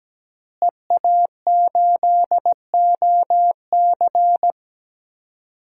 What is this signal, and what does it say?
Morse code, heard as a keyed tone, about 17 wpm, 700 Hz. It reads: EA8OC
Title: DXCC-100 Call Signs - 5 Characters - Repeated 3x - 17wpm